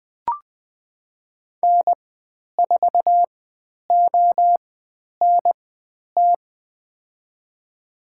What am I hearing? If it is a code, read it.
Morse code, heard as a keyed tone, about 20 wpm, spaced as for 10 wpm, 700 Hz. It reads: N4ONT
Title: 109 Lesson 18g - Letter m - Call Signs 20wpm